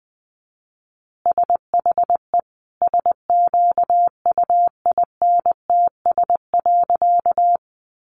Morse code, heard as a keyed tone, about 20 wpm, 700 Hz. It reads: SHE SQUINTS.
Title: Additional Easy Phrases 20wpm